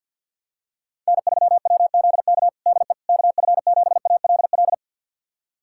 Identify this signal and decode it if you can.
Morse code, heard as a keyed tone, about 51 wpm, 700 Hz. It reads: N3WZK DE KF7ALL
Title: USA Callsign DE Callsign 50wpm